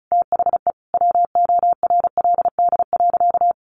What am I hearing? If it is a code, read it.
Morse code, heard as a keyed tone, about 34 wpm, 700 Hz. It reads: THE WORLD.
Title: Sentences from Top 200 Words 35wpm